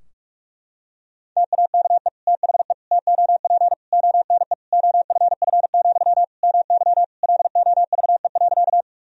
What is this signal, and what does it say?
Morse code, heard as a keyed tone, about 46 wpm, 700 Hz. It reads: TAKE THE TOP ONE OFF, MY LOVE.
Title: Sentences from Top 1000 Words - Review Part 2 - 45wpm